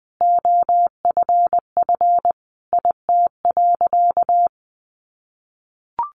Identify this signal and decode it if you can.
Morse code, heard as a keyed tone, about 20 wpm, 700 Hz. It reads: OFF IT.
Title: A Sound Of Thunder 20wpm